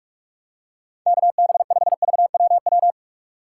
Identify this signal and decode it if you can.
Morse code, heard as a keyed tone, about 46 wpm, 700 Hz. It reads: KB5VWW
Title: USA Call Signs - 6 Characters - 3x Character Spacing - 45wpm